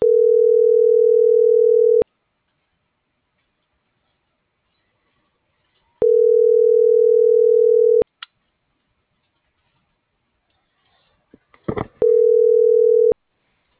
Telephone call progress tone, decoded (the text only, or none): ringback tone